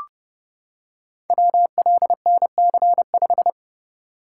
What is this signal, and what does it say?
Morse code, heard as a keyed tone, about 31 wpm, 700 Hz. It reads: WLNC5
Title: Code Groups - 5 Letter:Numbers - 30wpm